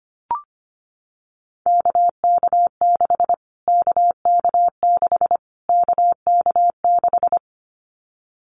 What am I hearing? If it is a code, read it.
Morse code, heard as a keyed tone, about 25 wpm, 700 Hz. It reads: KK6 KK6 KK6
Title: Callsign Prefixes for Top 25 DXCC - Repeated 3x - 25wpm